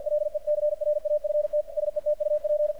Morse code, heard as a keyed tone, about 52 wpm, 600 Hz, with radio noise. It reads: LECRAFT SETUP
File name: archived / CRSnet